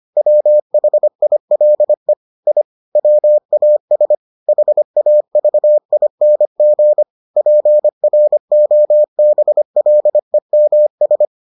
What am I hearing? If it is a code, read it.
Morse code, heard as a keyed tone, about 25 wpm, 600 Hz. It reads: WHILE I WAS HAVING PROBLEMS